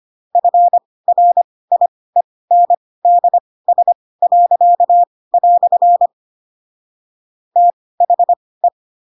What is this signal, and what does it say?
Morse code, heard as a keyed tone, about 25 wpm, spaced as for 18 wpm, 700 Hz. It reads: FRIENDS." THE
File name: Pinocchio / Pinocchio ch2 12 25 ik1hge